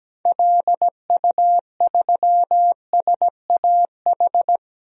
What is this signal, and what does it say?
Morse code, heard as a keyed tone, about 17 wpm, 700 Hz. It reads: LU3SAH
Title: DXCC-100 Call Signs - 6 Characters - 17wpm